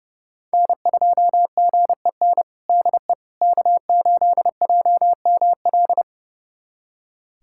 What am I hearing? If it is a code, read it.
Morse code, heard as a keyed tone, about 30 wpm, 700 Hz. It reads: N2GEN DE K8JML
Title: USA Callsign DE Callsign - 5 Characters - 30wpm